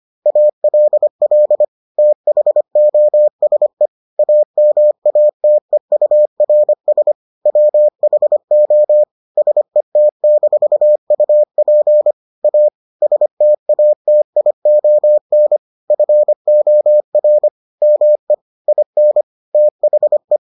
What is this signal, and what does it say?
Morse code, heard as a keyed tone, about 25 wpm, 600 Hz. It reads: ALL THOSE AMATEURS WHO SET-UP A STATION FOR ME IN THE